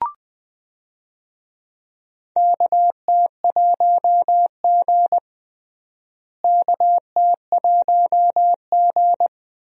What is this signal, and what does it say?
Morse code, heard as a keyed tone, about 20 wpm, 700 Hz. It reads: KT1G KT1G
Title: USA Call Signs - 4 Characters - Repeated 2x - 20wpm